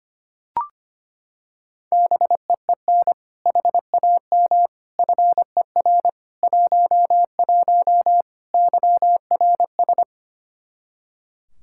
Morse code, heard as a keyed tone, about 25 wpm, 700 Hz. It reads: BEEN HAM FER 11 YRS
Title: Practice QSO - Exchange 3 - More Info - 25wpm